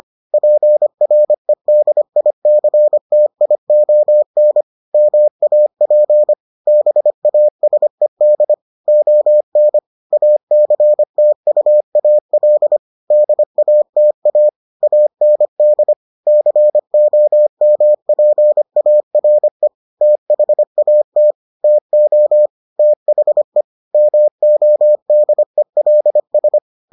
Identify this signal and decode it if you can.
Morse code, heard as a keyed tone, about 25 wpm, 600 Hz. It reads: PREDICTION MAP BASED ON ACTUAL DATA AND COMPARE THAT TO THE MODELS